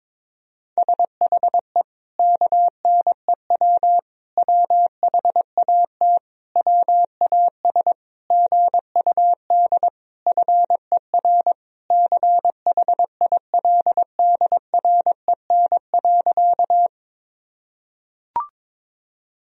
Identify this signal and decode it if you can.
Morse code, heard as a keyed tone, about 21 wpm, 700 Hz. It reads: SHE KNEW WHAT WAS GUD FER CHILDREN.
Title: Sentences from Top 400 Words and Abbreviations 22wpm